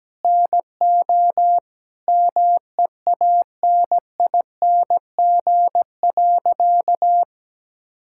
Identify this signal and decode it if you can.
Morse code, heard as a keyed tone, about 17 wpm, 700 Hz. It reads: NO MEANING.
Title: Sentences from Top 1000 Words - Review Part 6 - 17wpm